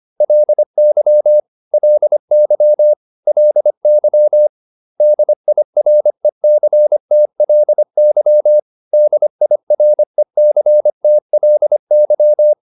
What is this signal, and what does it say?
Morse code, heard as a keyed tone, about 25 wpm, 600 Hz. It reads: LY LY LY DIRECTLY DIRECTLY
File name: LY8 25